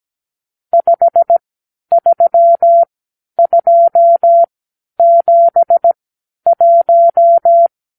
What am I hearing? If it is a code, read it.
Morse code, heard as a keyed tone, about 17 wpm, spaced as for 10 wpm, 680 Hz. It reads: 53271
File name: k7qo / Lesson 54 (9)